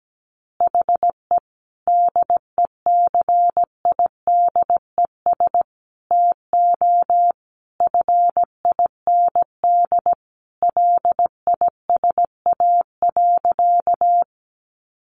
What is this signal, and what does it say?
Morse code, heard as a keyed tone, about 17 wpm, 700 Hz. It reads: HE DECIDES TO FIND LISA.
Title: Additional Easy Phrases 17wpm